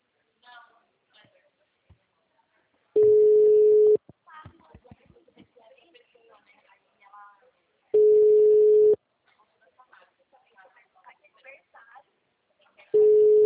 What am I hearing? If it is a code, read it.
Ringback tone (Europe, 425 Hz), with speech.